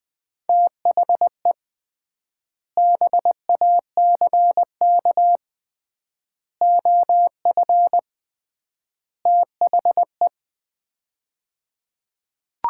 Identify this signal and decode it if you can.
Morse code, heard as a keyed tone, about 21 wpm, 700 Hz. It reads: THE BACK OF THE
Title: 3x Word-Spacing - Sets of 4 Words - 20wpm